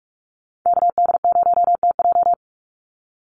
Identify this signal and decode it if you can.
Morse code, heard as a keyed tone, about 44 wpm, 700 Hz. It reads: KD0TJ
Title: USA Call Signs 45wpm